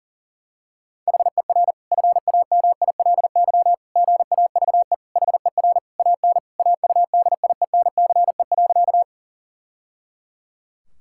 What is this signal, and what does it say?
Morse code, heard as a keyed tone, about 40 wpm, 700 Hz. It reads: HER FAMILY GAVE HER AN AUDIENCE.
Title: Sentences from Top 1000 Words - Review Part 5 - 40wpm